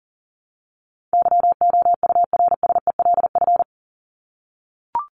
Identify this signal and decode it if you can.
Morse code, heard as a keyed tone, about 37 wpm, 700 Hz. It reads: YOURSELF
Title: Top 501-750 Words 40wpm